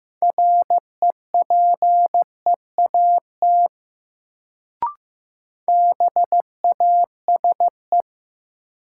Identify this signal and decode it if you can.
Morse code, heard as a keyed tone, about 15 wpm, 700 Hz. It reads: REPEAT BASE
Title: Sentences from Top 1600 Words - New Word Drill Alt - 15wpm